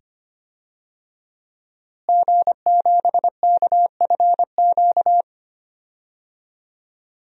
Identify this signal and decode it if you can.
Morse code, heard as a keyed tone, about 25 wpm, 700 Hz. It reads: G7KFQ
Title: DXCC-100 Call Signs - 5 Characters - Repeated 2x - 25wpm